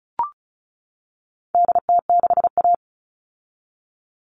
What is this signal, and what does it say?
Morse code, heard as a keyed tone, about 34 wpm, 700 Hz. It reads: DT6A
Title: All Cut Numbers - 4-Digit 35wpm